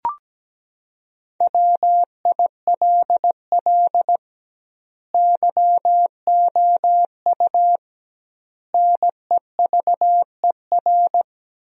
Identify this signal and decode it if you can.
Morse code, heard as a keyed tone, about 17 wpm, 700 Hz. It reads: WILL YOU NEVER
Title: 2x Word-Spacing - Sentences from Top 300 Words - 17wpm